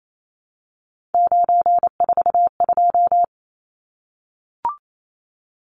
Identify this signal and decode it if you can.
Morse code, heard as a keyed tone, about 28 wpm, 700 Hz. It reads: 942
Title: Numbers - 3-Digit 28wpm